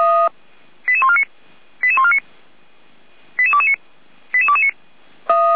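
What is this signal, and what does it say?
Five-tone selective call (ZVEI), heard as quick runs of five tones, tones of about 70 ms. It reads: 88149 88149 88229 88229
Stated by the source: ZVEI five-tone call received by SDR.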